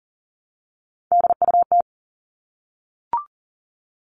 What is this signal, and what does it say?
Morse code, heard as a keyed tone, about 40 wpm, 700 Hz. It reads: DUT
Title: All Cut Numbers - 3-Digit 40wpm